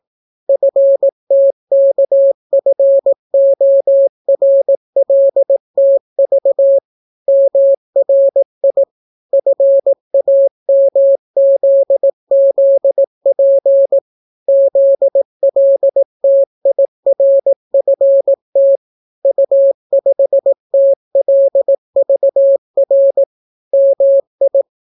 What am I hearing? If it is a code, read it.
Morse code, heard as a keyed tone, about 18 wpm, 550 Hz. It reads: FTKFORLTV MRI FAMZZP ZLTIRFT U5TLVR MI